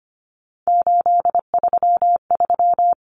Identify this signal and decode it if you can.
Morse code, heard as a keyed tone, about 25 wpm, 700 Hz. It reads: 833
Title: Numbers - 3-Digit 25wpm